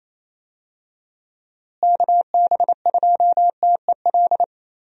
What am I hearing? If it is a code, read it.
Morse code, heard as a keyed tone, about 28 wpm, 700 Hz. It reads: KB2TEL
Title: USA Call Signs - 6 Characters - Repeated 3x - 28wpm